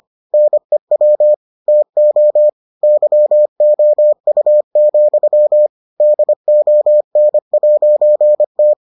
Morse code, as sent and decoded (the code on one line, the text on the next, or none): -. .-- / - --- / -.-- --- ..- --..-- / -.. --- -. .----. -
NEW TO YOU, DON'T